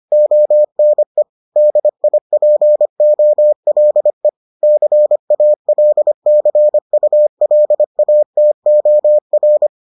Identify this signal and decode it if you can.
Morse code, heard as a keyed tone, about 25 wpm, 600 Hz. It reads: ONE DIPOLE CALCULATOR